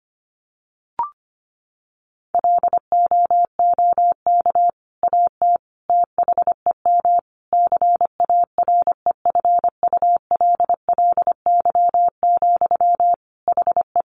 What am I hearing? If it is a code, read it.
Morse code, heard as a keyed tone, about 25 wpm, 700 Hz. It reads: LOOK AT THEM CAREFULLY, HE
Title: Sentences from Top 1300 Words 25wpm